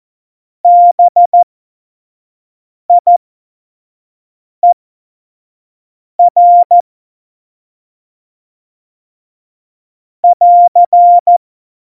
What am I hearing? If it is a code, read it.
Morse code, heard as a keyed tone, about 14 wpm, spaced as for 5 wpm, 700 Hz. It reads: BIER <AR>
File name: Morsekurs / Witze / Witz 11. Februar 2020, 25 F70